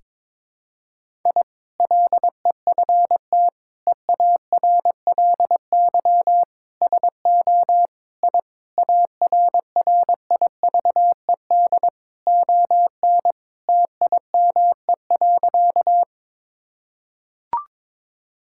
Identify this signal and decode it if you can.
Morse code, heard as a keyed tone, about 22 wpm, 700 Hz. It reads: I LEFT EARLY SO I ARRIVED ON TIME.
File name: practice / 22wpm / Sentences with 2-Word Phrases - 22wpm